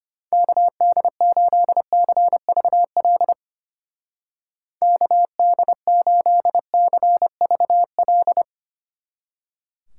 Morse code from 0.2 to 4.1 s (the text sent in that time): KD8CVL